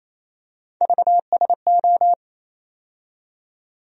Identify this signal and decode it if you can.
Morse code, heard as a keyed tone, about 28 wpm, 700 Hz. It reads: VSO
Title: USA Call Sign Postfixes - 3 Letters 28wpm